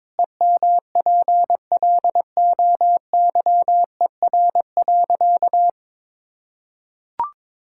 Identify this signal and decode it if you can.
Morse code, heard as a keyed tone, about 21 wpm, 700 Hz. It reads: EMPLOYER.
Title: Sentences from Top 1600 Words 22wpm